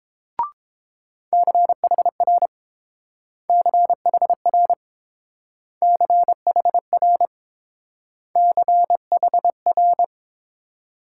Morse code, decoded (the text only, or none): CHR CHR CHR CHR